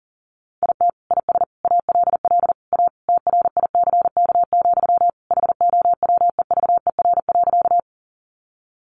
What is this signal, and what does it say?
Morse code, heard as a keyed tone, about 40 wpm, 700 Hz. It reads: IT IS ALL A TRICK, HOWEVER.
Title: Sentences from Top 1500 Words 40wpm